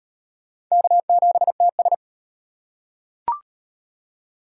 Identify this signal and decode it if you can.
Morse code, heard as a keyed tone, about 31 wpm, 700 Hz. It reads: K7TS